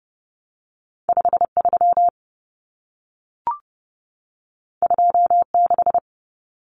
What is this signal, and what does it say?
Morse code, heard as a keyed tone, about 30 wpm, 700 Hz. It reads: 53 26
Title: Numbers - 2-Digit 30wpm